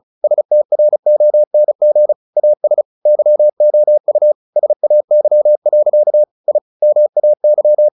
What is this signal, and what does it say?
Morse code, heard as a keyed tone, about 34 wpm, 600 Hz. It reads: STRONG AS YOU SAY. I MAY
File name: the old man and the sea / the old man and the sea 1 35wpm